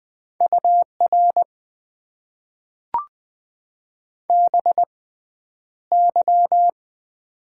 Morse code, heard as a keyed tone, about 20 wpm, spaced as for 10 wpm, 700 Hz. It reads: UR BY
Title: Top 90 QSO Elements - 6x Character Spacing - 20wpm